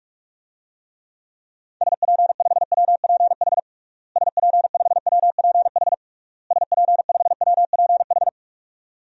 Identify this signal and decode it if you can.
Morse code, heard as a keyed tone, about 46 wpm, 700 Hz. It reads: SP5WPH SP5WPH SP5WPH
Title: DXCC-100 Call Signs - 6 Characters - Repeated 3x - 45wpm